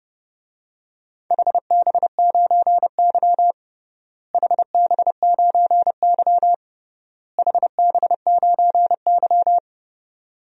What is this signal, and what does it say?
Morse code, heard as a keyed tone, about 30 wpm, 700 Hz. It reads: HB9Y HB9Y HB9Y
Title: DXCC-100 Call Signs - 4 Characters - Repeated 3x - 30wpm